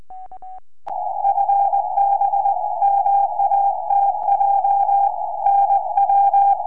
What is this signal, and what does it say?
Morse code, heard as a keyed tone, about 20 wpm, 800 Hz. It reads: FB MATJ NW
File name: sp9hwy 4m